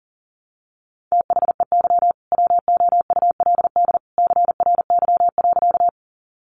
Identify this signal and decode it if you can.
Morse code, heard as a keyed tone, about 40 wpm, 700 Hz. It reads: THEY WOULD CRY.
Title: Sentences from Top 500 Words 40wpm